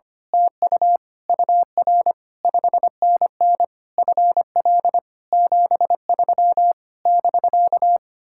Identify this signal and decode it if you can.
Morse code, heard as a keyed tone, about 24 wpm, 700 Hz. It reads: TU UR 5NN FL 73 <BK>
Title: Speed-Racing Common QSO Phrases - 25wpm